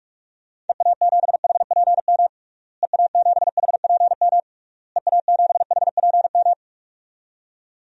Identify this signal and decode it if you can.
Morse code, heard as a keyed tone, about 46 wpm, 700 Hz. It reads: EA7HPM EA7HPM EA7HPM
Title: DXCC-100 Call Signs - 6 Characters - Repeated 3x - 45wpm